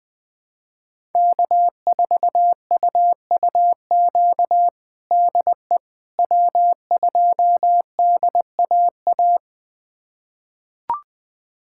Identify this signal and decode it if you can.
Morse code, heard as a keyed tone, about 20 wpm, 700 Hz. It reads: K4UUQ DE W2DAA
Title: USA Callsign DE Callsign - 5 Characters - 20wpm